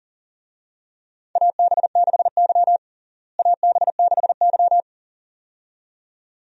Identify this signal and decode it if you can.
Morse code, heard as a keyed tone, about 40 wpm, 700 Hz. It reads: AB6Y AB6Y
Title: USA Call Signs - 4 Characters - Repeated 2x - 40wpm